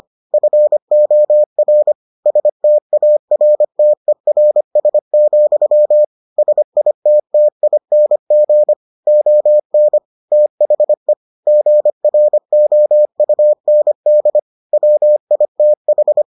Morse code, heard as a keyed tone, about 25 wpm, 600 Hz. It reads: FOR STARTERS, SITTING ON THE GROUND WITH